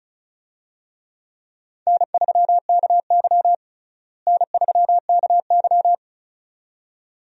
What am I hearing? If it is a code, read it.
Morse code, heard as a keyed tone, about 36 wpm, 700 Hz. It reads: N3KY N3KY
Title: USA Call Signs - 4 Characters - Repeated 2x - 35wpm